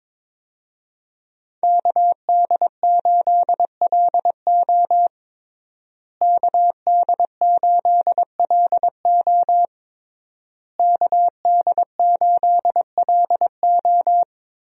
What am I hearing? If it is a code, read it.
Morse code, heard as a keyed tone, about 22 wpm, 700 Hz. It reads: KD8LO KD8LO KD8LO
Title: USA Call Signs - 5 Characters - Repeated 3x - 22wpm